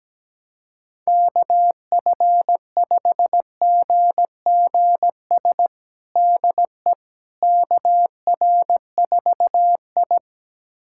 Morse code, heard as a keyed tone, about 17 wpm, 700 Hz. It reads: KF5GGS DE KR4I
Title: USA Callsign DE Callsign 17wpm